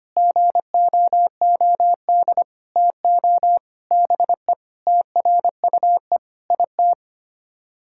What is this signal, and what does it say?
Morse code, heard as a keyed tone, about 25 wpm, 700 Hz. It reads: GOOD TO BE TRUE IT
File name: Sets of 9 Words - 25wpm